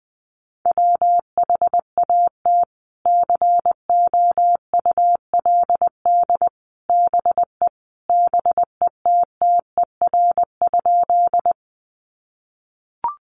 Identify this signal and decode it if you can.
Morse code, heard as a keyed tone, about 20 wpm, 700 Hz. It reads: WHAT COULD BE BETTER?
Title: Sentences from Top 400 Words 20wpm